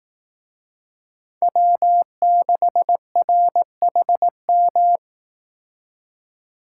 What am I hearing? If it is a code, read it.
Morse code, heard as a keyed tone, about 19 wpm, 700 Hz. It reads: W6RHM
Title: Speed Racing - USA 1x3 Call Signs 15wpm